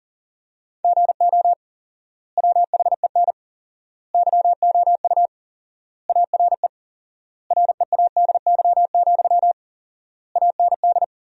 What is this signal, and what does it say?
Morse code, heard as a keyed tone, about 40 wpm, 700 Hz. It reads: GO WHEN YOU ARE READY, AND
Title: 4x Word-Spacing - Sentences from Top 500 Words - 40wpm